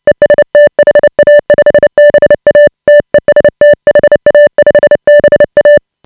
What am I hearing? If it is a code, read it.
Morse code, heard as a keyed tone, about 29 wpm, 590 Hz. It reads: EST HA5BA TEST HA5BA
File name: AUDIO2018 / HA5BA CW